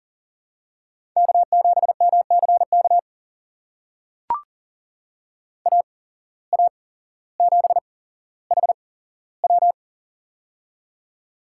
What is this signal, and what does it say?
Morse code, heard as a keyed tone, about 40 wpm, spaced as for 17 wpm, 700 Hz. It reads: K7MCK AA7HW
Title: USA Call Signs - 5 Characters - 8x Character Spacing - 40wpm